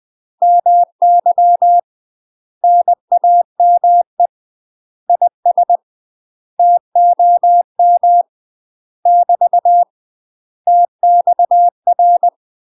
Morse code, heard as a keyed tone, about 20 wpm, 700 Hz. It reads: MY NAME IS TOM <BT> TXR
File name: UK MASTER QSOs TRAINING/20WPM